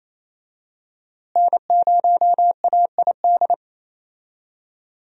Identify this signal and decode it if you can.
Morse code, heard as a keyed tone, about 28 wpm, 700 Hz. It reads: N0AID
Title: USA Call Signs - 5 Characters - Repeated 2x - 28wpm